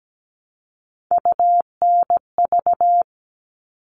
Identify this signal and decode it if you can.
Morse code, heard as a keyed tone, about 17 wpm, 700 Hz. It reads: UNV
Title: All Cut Numbers - 3-Digit 17wpm